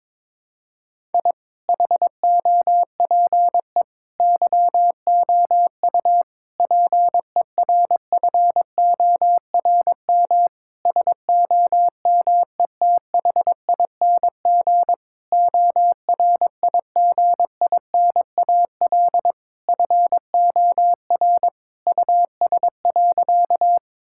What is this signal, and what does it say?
Morse code, heard as a keyed tone, about 21 wpm, 700 Hz. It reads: I HOPE YOU PERFORM SOMETHING ORIGINAL FOR US.
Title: Sentences from Top 1000 Words - Review Part 6 - 22wpm